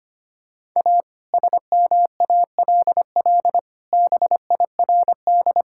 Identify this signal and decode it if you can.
Morse code, heard as a keyed tone, about 25 wpm, 700 Hz. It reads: A SMALL BIRD
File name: practice / 25wpm / Sentences with 2-Word Phrases - 25wpm